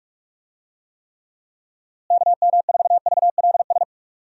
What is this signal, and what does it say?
Morse code, heard as a keyed tone, about 48 wpm, 700 Hz. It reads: KM4VLS
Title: USA Call Signs - 6 Characters - Repeated 3x - 45wpm